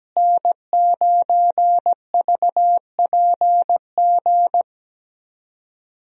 Morse code, heard as a keyed tone, about 17 wpm, 700 Hz. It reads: N9VPG
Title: USA Call Signs - 5 Characters - Repeated 2x - 17wpm